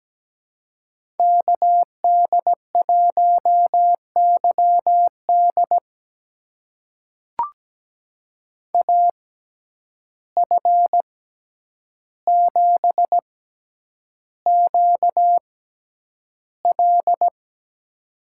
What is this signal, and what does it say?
Morse code, heard as a keyed tone, about 17 wpm, spaced as for 9 wpm, 700 Hz. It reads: KD1YD AF7QL
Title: USA Call Signs - 5 Characters - 6x Character Spacing - 17wpm